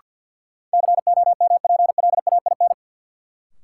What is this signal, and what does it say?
Morse code, heard as a keyed tone, about 51 wpm, 700 Hz. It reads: COMPLAIN
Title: Sentences from Top 1300 Words - New Word Drill - 50wpm